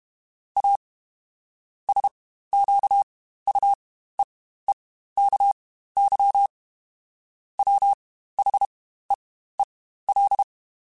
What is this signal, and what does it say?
Morse code, heard as a keyed tone, about 32 wpm, spaced as for 15 wpm, 800 Hz. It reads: A SQUEEKY WHEEL
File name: Spaced / Chuckles Spaced-mp3s / Squeaky sp @30